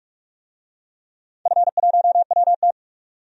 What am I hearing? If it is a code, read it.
Morse code, heard as a keyed tone, about 41 wpm, 700 Hz. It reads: F1WT
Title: DXCC-100 Call Signs - 4 Characters - 45wpm